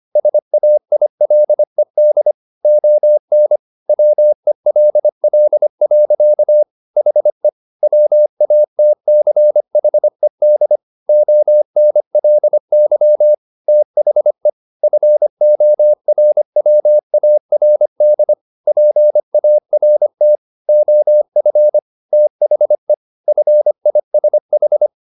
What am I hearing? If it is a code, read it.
Morse code, heard as a keyed tone, about 25 wpm, 600 Hz. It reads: SAILED ON WELL. HE WATCHED ONLY THE FORWARD PART OF THE FISH